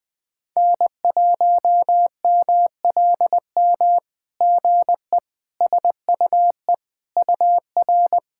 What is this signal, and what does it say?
Morse code, heard as a keyed tone, about 20 wpm, 700 Hz. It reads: N1MLM GE SUE UR
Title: SOTA QSO Practice 20wpm